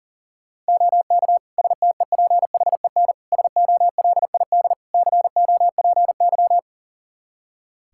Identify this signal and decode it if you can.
Morse code, heard as a keyed tone, about 40 wpm, 700 Hz. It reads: OK STEPHEN SOLID COPY
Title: K1USN SST QSO Practice 40wpm